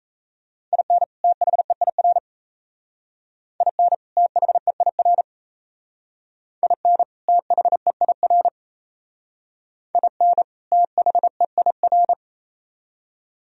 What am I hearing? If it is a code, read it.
Morse code, heard as a keyed tone, about 35 wpm, 700 Hz. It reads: IN THEIR IN THEIR IN THEIR IN THEIR
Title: Speed-Racing - Sets of 2 Words 28wpm